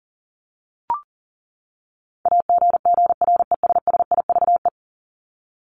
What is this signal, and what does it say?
Morse code, heard as a keyed tone, about 37 wpm, 700 Hz. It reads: AGGRESSIVE
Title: Top 1501-1750 Words 40wpm